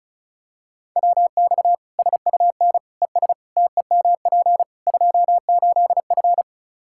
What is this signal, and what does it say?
Morse code, heard as a keyed tone, about 36 wpm, 700 Hz. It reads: WX SUN ES TEMP 28F
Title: Practice QSO - Exchange 2 - Info - 35wpm